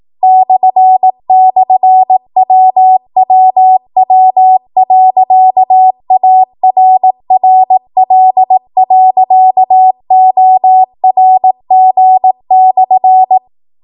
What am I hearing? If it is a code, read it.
Morse code, heard as a keyed tone, about 18 wpm, 750 Hz. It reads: //WWW.ARRL.ORG/
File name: Morse / 18WPM